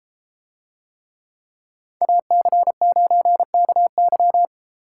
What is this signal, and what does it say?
Morse code, heard as a keyed tone, about 33 wpm, 700 Hz. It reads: AC9KY